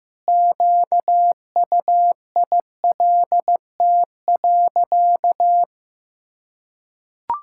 Morse code, heard as a keyed tone, about 15 wpm, 700 Hz. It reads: QUILT.
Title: Pangram Challenge v2 - 15wpm